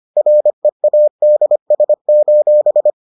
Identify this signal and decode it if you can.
Morse code, heard as a keyed tone, about 25 wpm, 600 Hz. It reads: READS: